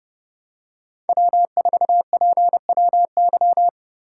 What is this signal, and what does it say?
Morse code, heard as a keyed tone, about 32 wpm, 700 Hz. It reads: W4PWY
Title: USA Call Signs - 5 Characters - 3x Character Spacing - 30wpm